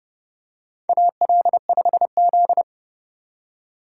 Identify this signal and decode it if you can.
Morse code, heard as a keyed tone, about 31 wpm, 700 Hz. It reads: AL5Z
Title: USA Call Signs - 4 Characters - 2x Character Spacing - 30wpm